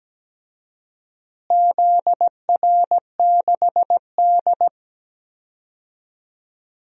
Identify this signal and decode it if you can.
Morse code, heard as a keyed tone, about 17 wpm, 700 Hz. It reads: ZR6D
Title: DXCC-100 Call Signs - 4 Characters - Repeated 2x - 17wpm